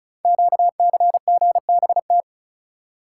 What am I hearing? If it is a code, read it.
Morse code, heard as a keyed tone, about 36 wpm, 690 Hz. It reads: QCGBT
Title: Code Groups - 5 Letter:Numbers - 35wpm